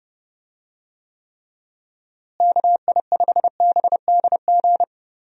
Morse code, heard as a keyed tone, about 32 wpm, 700 Hz. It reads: KI5BDG
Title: USA Call Signs - 6 Characters - Repeated 2x - 30wpm